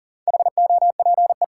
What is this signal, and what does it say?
Morse code, heard as a keyed tone, about 40 wpm, 700 Hz. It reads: HOPE
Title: Sentences from Top 1000 Words - New Word Drill - 40wpm